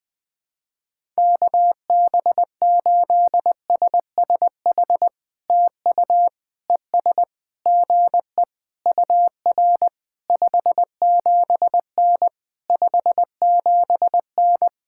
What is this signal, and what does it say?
Morse code, heard as a keyed tone, about 20 wpm, 700 Hz. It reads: KB8SSH TU ES GE UR 57N 57N
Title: POTA QSO Practice - 20wpm